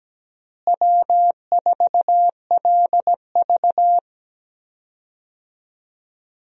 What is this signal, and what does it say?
Morse code, heard as a keyed tone, about 17 wpm, 700 Hz. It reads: W4LV
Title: USA Call Signs - 4 Characters - Repeated 2x - 17wpm